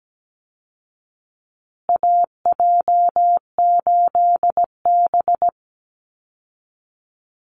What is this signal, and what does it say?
Morse code, heard as a keyed tone, about 17 wpm, 700 Hz. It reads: AJ8B
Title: Call Signs Repeated 2x - 17wpm